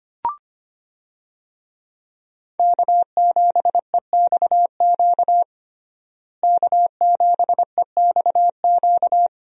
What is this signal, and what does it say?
Morse code, heard as a keyed tone, about 25 wpm, 700 Hz. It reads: K7EXQ K7EXQ
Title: USA Call Signs - 5 Characters - Repeated 3x - 25wpm